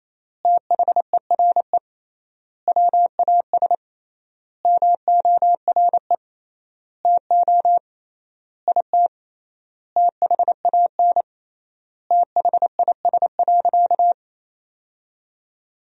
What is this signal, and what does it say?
Morse code, heard as a keyed tone, about 28 wpm, 700 Hz. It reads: THERE WAS MORE TO IT THAN THIS.
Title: 3x Word-Spacing - Sentences from Top 100 Words - 28wpm